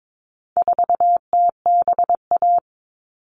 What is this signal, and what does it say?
Morse code, heard as a keyed tone, about 22 wpm, 700 Hz. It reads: 4TBA
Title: All Cut Numbers - 4-Digit 22wpm